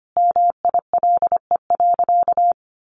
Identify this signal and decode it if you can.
Morse code, heard as a keyed tone, about 25 wpm, 700 Hz. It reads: MILE.
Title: Sentences from Top 1300 Words 25wpm